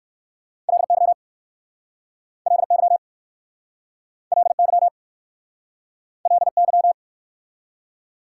Morse code, heard as a keyed tone, about 59 wpm, 700 Hz. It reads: LY LY LY LY